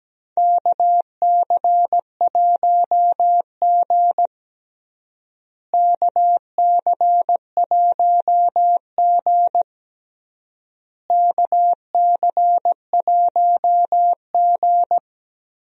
Morse code, heard as a keyed tone, about 17 wpm, 700 Hz. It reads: KC1G KC1G KC1G
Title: USA Call Signs - 4 Characters - Repeated 3x - 17wpm